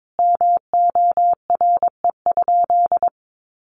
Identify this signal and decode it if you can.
Morse code, heard as a keyed tone, about 22 wpm, 700 Hz. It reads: MORE?
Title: Sentences from Top 1300 Words 22wpm